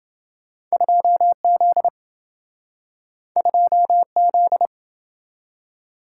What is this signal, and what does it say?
Morse code, heard as a keyed tone, about 27 wpm, 700 Hz. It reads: 2Z 2Z